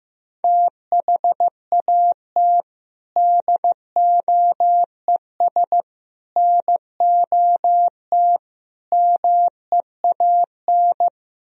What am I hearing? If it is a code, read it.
Morse code, heard as a keyed tone, about 15 wpm, 700 Hz. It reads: THAT DOES NOT MEAN
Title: Speed-Racing - Sets of 4 Words 15wpm